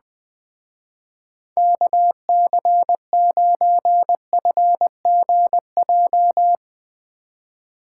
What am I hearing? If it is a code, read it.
Morse code, heard as a keyed tone, about 22 wpm, 700 Hz. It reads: KC9FGJ
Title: Speed Racing - USA Call Signs 20wpm